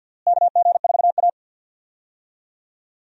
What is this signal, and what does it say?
Morse code, heard as a keyed tone, about 51 wpm, 700 Hz. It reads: KG4A
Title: USA Call Signs - 4 Characters - Repeated 2x - 50wpm